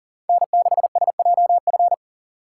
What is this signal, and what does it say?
Morse code, heard as a keyed tone, about 49 wpm, 700 Hz. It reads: N6SJF
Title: Speed Racing - USA 1x3 Call Signs 40wpm